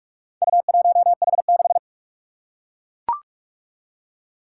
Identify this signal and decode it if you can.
Morse code, heard as a keyed tone, about 42 wpm, 700 Hz. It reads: U1H6
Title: Four Challenging Letter-Number Combinations 45wpm